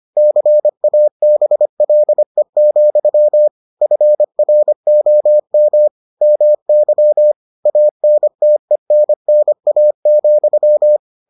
Morse code, heard as a keyed tone, about 25 wpm, 600 Hz. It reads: CABLE, FROM MY ANTENNA,